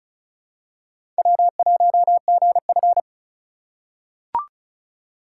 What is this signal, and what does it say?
Morse code, heard as a keyed tone, about 34 wpm, 700 Hz. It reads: W1GF
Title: USA Call Signs - 4x Character Spacing - 35wpm